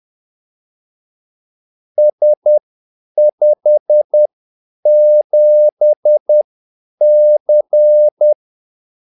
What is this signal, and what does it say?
Morse code, heard as a keyed tone, about 10 wpm, spaced as for 8 wpm, 600 Hz. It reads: S57C
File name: Callsigns-10-8wpm